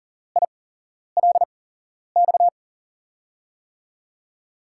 Morse code, heard as a keyed tone, about 39 wpm, spaced as for 11 wpm, 700 Hz. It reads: ILX
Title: USA Call Signs - 6 Characters - 8x Character Spacing - 40wpm